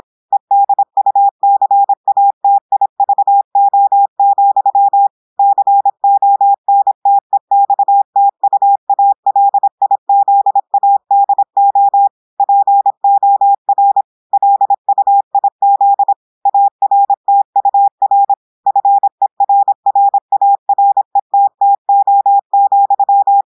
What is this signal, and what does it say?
Morse code, heard as a keyed tone, about 26 wpm, 800 Hz. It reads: EDUCATIVO, CONTEXTUALIZADO POR LUIZ ARTUR FERRARETTO,
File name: audios / portugais / portugais texte en clair 1 vitesse 26 mots minutes